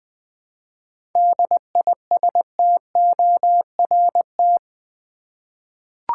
Morse code, heard as a keyed tone, about 20 wpm, 700 Hz. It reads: DISTORT
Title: 7 Letter Words - First 3 Letters Given - 20wpm